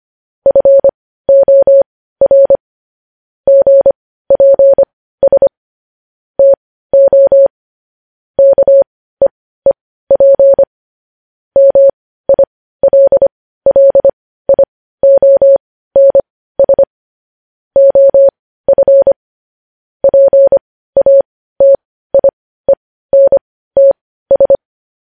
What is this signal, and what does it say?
Morse code, heard as a keyed tone, about 25 wpm, spaced as for 15 wpm, 550 Hz. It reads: FOR GPS TO KEEP MILLIONS OF PATIENTS
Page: news-headlines-in-morse-code-at-15-wpm